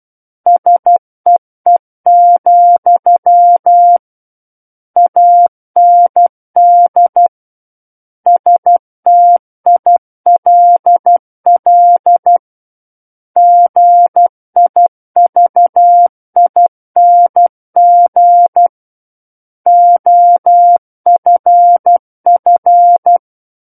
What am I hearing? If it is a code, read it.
Morse code, heard as a keyed tone, about 12 wpm, 700 Hz. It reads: SEE, AND STILL GIVING OFF
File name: Book One / 03-Chapter 3 - 12 WPM